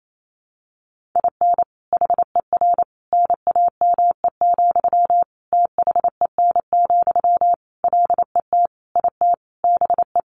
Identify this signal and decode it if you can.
Morse code, heard as a keyed tone, about 28 wpm, 700 Hz. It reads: IN HER NAME, THEN, LET IT BE
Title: Sentences from Top 500 Words 28wpm